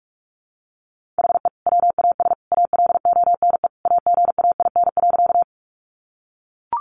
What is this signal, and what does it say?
Morse code, heard as a keyed tone, about 44 wpm, 700 Hz. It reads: HE WAS ALONE AGAIN.
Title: Sentences from Top 1000 Words 45wpm